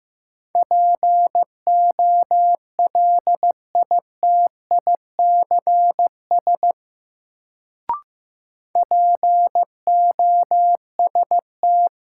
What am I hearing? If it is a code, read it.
Morse code, heard as a keyed tone, about 15 wpm, 700 Hz. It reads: POLITICS POST
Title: Sentences from Top 1000 Words - New Word Drill - 15wpm